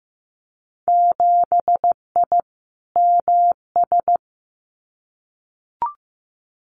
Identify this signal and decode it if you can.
Morse code, heard as a keyed tone, about 15 wpm, 700 Hz. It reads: 7I MS
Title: Winter Field Day Exchanges Encore 15wpm